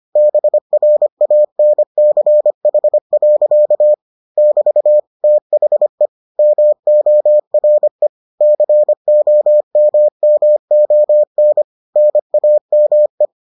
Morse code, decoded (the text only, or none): BRANCH. <BT> THE MORE COMMON NAME